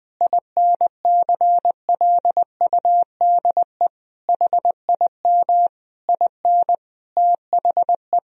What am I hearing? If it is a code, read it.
Morse code, heard as a keyed tone, about 20 wpm, 700 Hz. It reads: INCLUDE HIM IN THE
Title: Sentences from Top 1000 Words - Review Part 3 - 20wpm